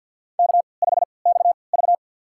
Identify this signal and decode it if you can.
Morse code, heard as a keyed tone, about 52 wpm, spaced as for 29 wpm, 700 Hz. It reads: K5XV
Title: USA Call Signs - 3x Character Spacing - 50wpm